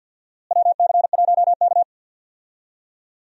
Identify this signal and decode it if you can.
Morse code, heard as a keyed tone, about 52 wpm, 700 Hz. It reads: WC1K
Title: USA Call Signs - 4 Characters - Repeated 2x - 50wpm